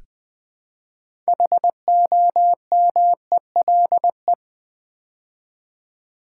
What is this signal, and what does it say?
Morse code, heard as a keyed tone, about 20 wpm, 700 Hz. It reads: HOMELE
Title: Build A Word - 3 Letters At A Time - 8-Letter Words - 20wpm